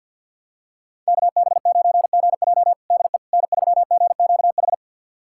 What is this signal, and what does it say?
Morse code, heard as a keyed tone, about 52 wpm, 690 Hz. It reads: KB9GJ DE N3GQH
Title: USA Callsign DE Callsign - 5 Characters - 50wpm